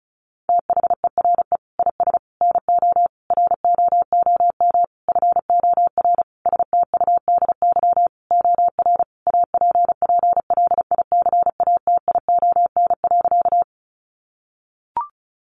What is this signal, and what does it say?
Morse code, heard as a keyed tone, about 34 wpm, 700 Hz. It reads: THERE IS NO ROOM FOR STUDY OR APPLICATION.
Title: Sentences from Top 1200 Words 35wpm